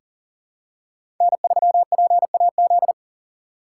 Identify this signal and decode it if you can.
Morse code, heard as a keyed tone, about 41 wpm, 700 Hz. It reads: N3PAZ
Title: USA Call Signs - 5 Characters - 6x Character Spacing - 40wpm